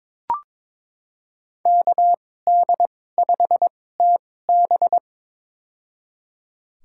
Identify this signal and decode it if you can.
Morse code, heard as a keyed tone, about 22 wpm, spaced as for 16 wpm, 700 Hz. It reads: KD5TB
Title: USA Call Signs - 5 Characters - 2x Character Spacing - 22wpm